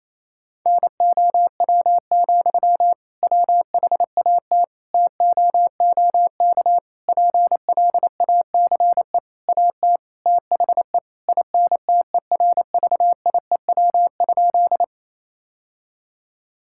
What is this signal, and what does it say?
Morse code, heard as a keyed tone, about 28 wpm, 700 Hz. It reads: NOW, WHAT TOOK PLACE AT THE INTERVIEW?
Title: Sentences from Top 1600 Words 28wpm